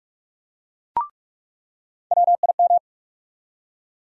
Morse code, heard as a keyed tone, about 46 wpm, 700 Hz. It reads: WIM
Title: USA Call Sign Postfixes - 3 Letters 45wpm